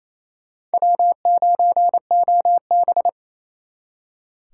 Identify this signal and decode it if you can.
Morse code, heard as a keyed tone, about 37 wpm, 700 Hz. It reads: W9OB